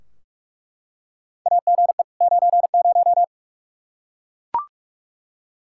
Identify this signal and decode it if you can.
Morse code, heard as a keyed tone, about 46 wpm, 700 Hz. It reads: AGE 90
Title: Practice QSO - Exchange 3 - More Info - 45wpm